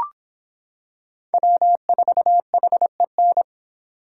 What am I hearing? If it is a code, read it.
Morse code, heard as a keyed tone, about 26 wpm, 700 Hz. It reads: W4HEN